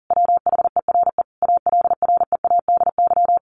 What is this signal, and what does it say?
Morse code, heard as a keyed tone, about 37 wpm, 700 Hz. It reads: WHERE ALREADY
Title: A Sound Of Thunder 40wpm